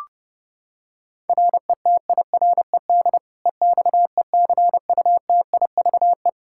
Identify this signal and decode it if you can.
Morse code, heard as a keyed tone, about 30 wpm, 700 Hz. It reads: RETIRED EXECUTIVE
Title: Practice QSO - Exchange 3 - More Info - 30wpm